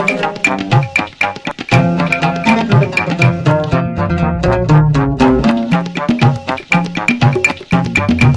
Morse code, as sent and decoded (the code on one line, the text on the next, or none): none